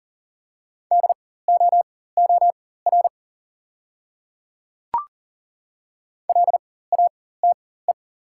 Morse code, heard as a keyed tone, about 40 wpm, spaced as for 19 wpm, 690 Hz. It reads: DOOR LATE
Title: 4x Character Spacing - Top 300 Words - 40wpm